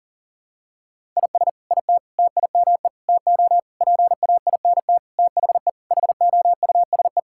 Morse code, heard as a keyed tone, about 40 wpm, 700 Hz. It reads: IS IT TIME TO PAINT THE HOUSE
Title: Sentences from Top 1000 Words - Review Part 6 - 40wpm